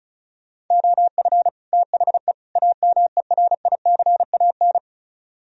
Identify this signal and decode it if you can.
Morse code, heard as a keyed tone, about 43 wpm, 700 Hz. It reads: OF THE AMERICAN
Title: Speed-Racing - Sets of 3 Words 35wpm